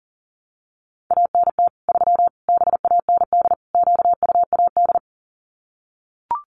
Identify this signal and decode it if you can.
Morse code, heard as a keyed tone, about 40 wpm, 700 Hz. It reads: ANT 3 BAND QUAD
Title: Common QSO Phrases 40wpm